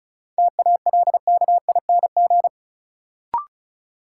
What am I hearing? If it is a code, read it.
Morse code, heard as a keyed tone, about 34 wpm, 700 Hz. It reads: TALKING
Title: Sets of 2 Words - New Word Drill - 35wpm